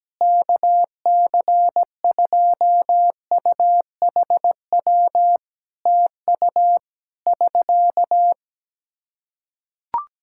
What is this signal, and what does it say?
Morse code, heard as a keyed tone, about 17 wpm, 700 Hz. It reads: KC2UHW TU <SK>
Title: Practice QSO - Exchange 4 - Ending - 17wpm